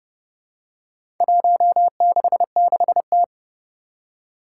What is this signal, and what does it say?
Morse code, heard as a keyed tone, about 30 wpm, 700 Hz. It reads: 166T
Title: Four Challenging Letter-Number Combinations 30wpm